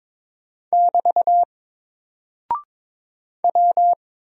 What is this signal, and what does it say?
Morse code, heard as a keyed tone, about 22 wpm, 700 Hz. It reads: <BT> W
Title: Single Character (Letter:Number:Punctuation:Prosign) 22wpm